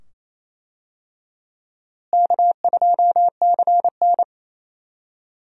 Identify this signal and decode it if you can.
Morse code, heard as a keyed tone, about 28 wpm, 700 Hz. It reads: K2CN
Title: USA Call Signs - 4 Characters - Repeated 2x - 28wpm